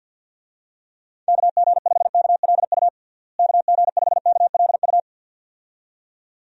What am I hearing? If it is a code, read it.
Morse code, heard as a keyed tone, about 49 wpm, 700 Hz. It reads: KG5KLU KG5KLU
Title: USA Call Signs - 6 Characters - Repeated 2x - 50wpm